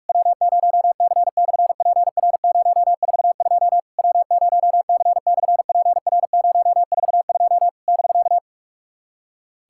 Morse code, heard as a keyed tone, about 44 wpm, 690 Hz. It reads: W0C/PR042 W0C/PR042 <BK>
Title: SOTA QSO Practice 45wpm